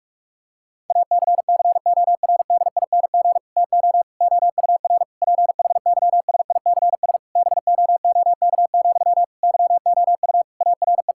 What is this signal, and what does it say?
Morse code, heard as a keyed tone, about 46 wpm, 690 Hz. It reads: ACCORDING TO OUR PHYSICS BOOK, YOU ARE
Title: Sentences from Top 1000 Words - Review Part 5 - 45wpm